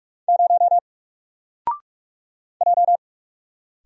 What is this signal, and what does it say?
Morse code, heard as a keyed tone, about 45 wpm, 700 Hz. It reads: 0 J
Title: Single Letter-Number Alt 45wpm